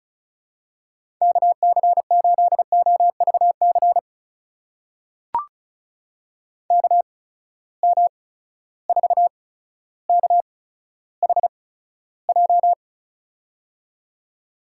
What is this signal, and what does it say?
Morse code, heard as a keyed tone, about 36 wpm, spaced as for 15 wpm, 700 Hz. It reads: KC8OVC KM4KHJ
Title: USA Call Signs - 6 Characters - 8x Character Spacing - 35wpm